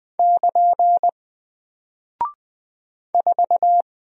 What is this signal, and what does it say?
Morse code, heard as a keyed tone, about 20 wpm, 700 Hz. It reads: <KN> 4
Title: Single Character (Letter:Number:Punctuation:Prosign) 20wpm